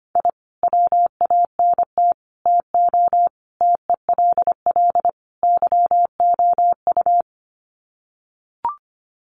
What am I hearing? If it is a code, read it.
Morse code, heard as a keyed tone, about 25 wpm, 700 Hz. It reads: I WANT TO TELL YOU
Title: Sets of 5 Words Encore 25wpm